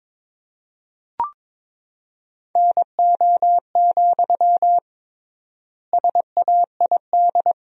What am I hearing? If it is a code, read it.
Morse code, heard as a keyed tone, about 22 wpm, 700 Hz. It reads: NO, SAID
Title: 3x Word-Spacing - Sentences from Top 300 Words - 22wpm